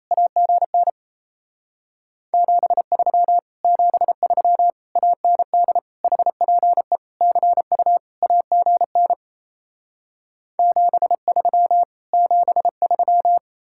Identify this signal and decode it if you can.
Morse code, heard as a keyed tone, about 37 wpm, 700 Hz. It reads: AGN 73 73 AND HPE CU AGN 73 73